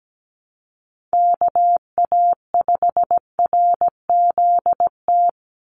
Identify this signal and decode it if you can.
Morse code, heard as a keyed tone, about 17 wpm, 700 Hz. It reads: KA5RZT
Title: USA Call Signs 17wpm